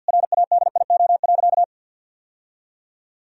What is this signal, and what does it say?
Morse code, heard as a keyed tone, about 50 wpm, 700 Hz. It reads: RADIO.
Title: Sentences from Top 1000 Words - Review Part 3 - 50wpm